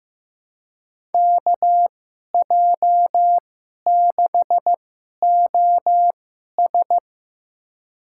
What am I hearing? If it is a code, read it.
Morse code, heard as a keyed tone, about 15 wpm, spaced as for 11 wpm, 700 Hz. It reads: KJ6OS
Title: USA Call Signs - 5 Characters - 2x Character Spacing - 15wpm